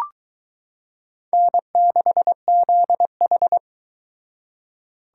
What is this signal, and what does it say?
Morse code, heard as a keyed tone, about 21 wpm, 700 Hz. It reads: N6ZH